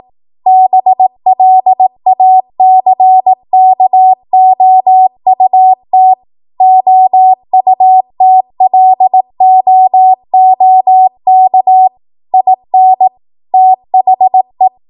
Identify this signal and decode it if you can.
Morse code, heard as a keyed tone, about 18 wpm, 750 Hz. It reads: BLACKOUT OUTLOOK IN THE
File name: Archive / 18 WPM / 18WPM